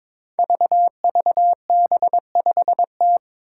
Morse code, heard as a keyed tone, about 22 wpm, 700 Hz. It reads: VVB5T
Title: Five Challenging Letter-Number Combinations 22wpm